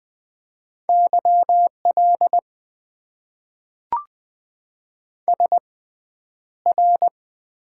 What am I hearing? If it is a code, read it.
Morse code, heard as a keyed tone, about 20 wpm, spaced as for 10 wpm, 700 Hz. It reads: YL SR
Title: Top 90 QSO Elements - 6x Character Spacing - 20wpm